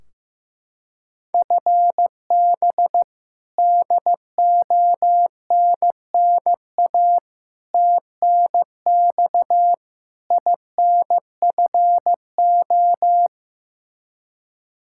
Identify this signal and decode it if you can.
Morse code, heard as a keyed tone, about 15 wpm, 700 Hz. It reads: FB DONNA TNX INFO
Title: Practice QSO - Exchange 3 - More Info - 15wpm